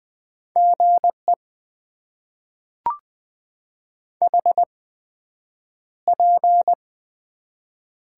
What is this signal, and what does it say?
Morse code, heard as a keyed tone, about 20 wpm, spaced as for 9 wpm, 700 Hz. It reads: GE HP
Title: Top 90 QSO Elements - 8x Character Spacing - 20wpm